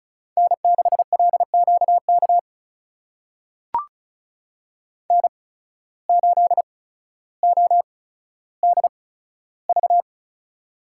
Morse code, heard as a keyed tone, about 36 wpm, spaced as for 15 wpm, 700 Hz. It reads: N6LQK N8ODV
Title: USA Call Signs - 8x Character Spacing - 35wpm